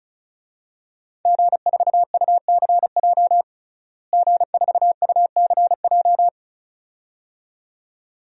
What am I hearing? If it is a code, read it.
Morse code, heard as a keyed tone, about 36 wpm, 700 Hz. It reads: G4UCJ G4UCJ
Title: DXCC-100 Call Signs - 5 Characters - Repeated 2x - 35wpm